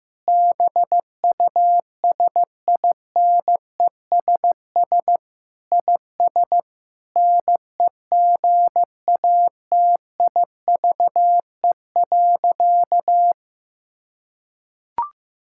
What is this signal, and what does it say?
Morse code, heard as a keyed tone, about 15 wpm, 700 Hz. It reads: BUSINESS IS NEGATIVE.
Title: Sentences from Top 1000 Words - Review Part 6 - 15wpm